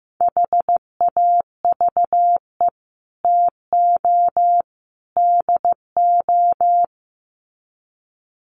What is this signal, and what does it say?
Morse code, heard as a keyed tone, about 15 wpm, 700 Hz. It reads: HAVE TO DO
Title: Sets of 3 Words 15wpm